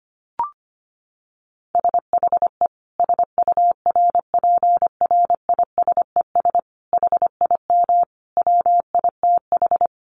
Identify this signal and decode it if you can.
Morse code, heard as a keyed tone, about 25 wpm, 700 Hz. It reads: SHE SURPRISES HIM WITH
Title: Additional Easy Phrases 25wpm